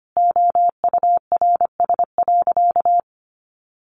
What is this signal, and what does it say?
Morse code, heard as a keyed tone, about 25 wpm, 700 Hz. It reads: OURS.
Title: Sentences from Top 1300 Words 25wpm